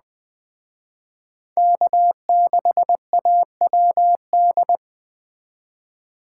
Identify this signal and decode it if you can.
Morse code, heard as a keyed tone, about 20 wpm, 700 Hz. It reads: K6AWD
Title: Speed Racing - USA Call Signs 17wpm